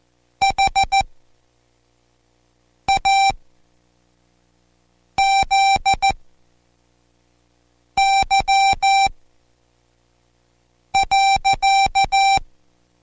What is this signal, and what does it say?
Morse code, heard as a keyed tone, about 14 wpm, spaced as for 4 wpm, 780 Hz. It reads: HAZY.